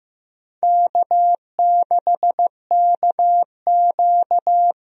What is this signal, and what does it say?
Morse code, heard as a keyed tone, about 15 wpm, 700 Hz. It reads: K6KQ
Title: USA Call Signs - 4 Characters - 15wpm